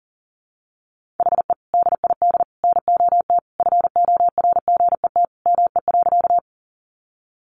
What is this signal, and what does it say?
Morse code, heard as a keyed tone, about 40 wpm, 700 Hz. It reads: HE DID NOT FORGET ME.
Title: Sentences from Top 1200 Words 40wpm